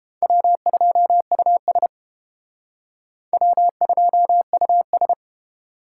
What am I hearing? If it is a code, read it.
Morse code, heard as a keyed tone, about 30 wpm, 700 Hz. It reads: W2US W2US